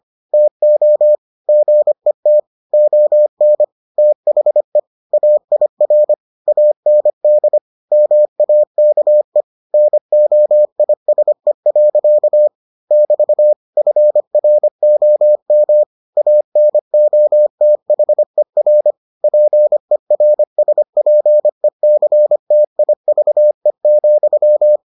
Morse code, heard as a keyed tone, about 25 wpm, 600 Hz. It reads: TO GET ON THE AIR AND MAKE NOISE. <BT> FROM ANOTHER PERSPECTIVE,